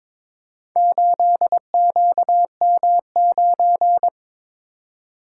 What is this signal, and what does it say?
Morse code, heard as a keyed tone, about 21 wpm, 700 Hz. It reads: 8QM9
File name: practice / 22wpm / Four Letter-Number Combinations - 22wpm